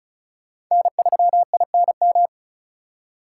Morse code, heard as a keyed tone, about 45 wpm, 700 Hz. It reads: N3INM